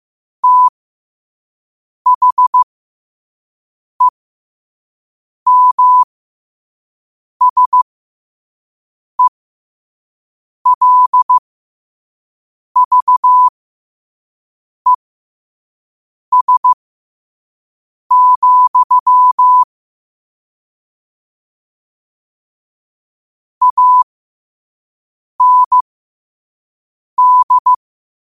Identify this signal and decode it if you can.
Morse code, heard as a keyed tone, about 15 wpm, spaced as for 5 wpm, 1000 Hz. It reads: THEMSELVES, AND
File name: qotdfiles / 5wpm